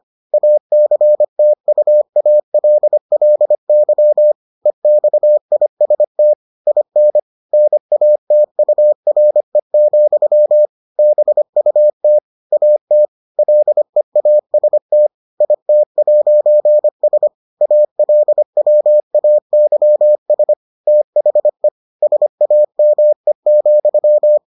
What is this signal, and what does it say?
Morse code, heard as a keyed tone, about 25 wpm, 600 Hz. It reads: ACTUALLY EXIST IN NATURE, BUT AT LEAST IT'S ALWAYS THE SAME,